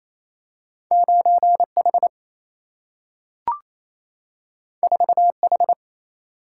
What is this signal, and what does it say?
Morse code, heard as a keyed tone, about 28 wpm, 700 Hz. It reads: 9H 4H
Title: Single-Letter Single-Number 28wpm